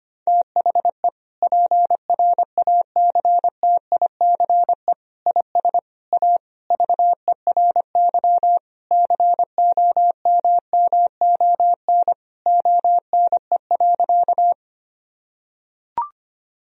Morse code, heard as a keyed tone, about 25 wpm, 700 Hz. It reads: THE PRACTICE IS A VERY COMMON ONE.
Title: Sentences from Top 1000 Words - Review Part 4 - 25wpm